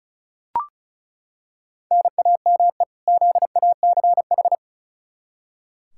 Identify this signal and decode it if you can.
Morse code, heard as a keyed tone, about 36 wpm, 700 Hz. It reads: NAME ZACH
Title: K1USN SST QSO Practice 35wpm